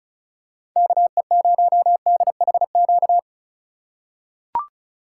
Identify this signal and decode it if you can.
Morse code, heard as a keyed tone, about 36 wpm, 700 Hz. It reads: KE0DHQ
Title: USA Call Signs - 6 Characters - 4x Character Spacing - 35wpm